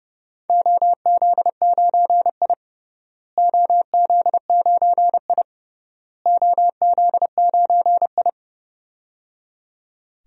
Morse code, heard as a keyed tone, about 30 wpm, 700 Hz. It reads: OZ9I OZ9I OZ9I
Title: DXCC-100 Call Signs - 4 Characters - Repeated 3x - 30wpm